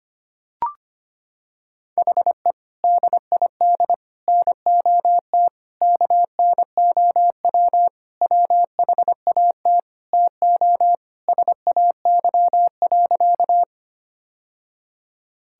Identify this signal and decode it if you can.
Morse code, heard as a keyed tone, about 25 wpm, 700 Hz. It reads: HE DID NOT KNOW WHAT TO SAY.
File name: practice / 25wpm / Sentences with 3-Word Phrases - 25wpm